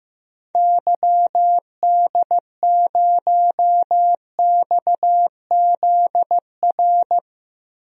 Morse code, heard as a keyed tone, about 15 wpm, 700 Hz. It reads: YD0XZR
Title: DXCC-100 Call Signs - 6 Characters - Repeated 2x - 15wpm